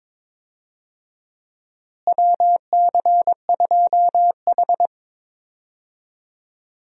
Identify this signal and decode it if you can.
Morse code, heard as a keyed tone, about 22 wpm, 700 Hz. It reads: WC2H
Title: USA Call Signs - 4 Characters - Repeated 2x - 22wpm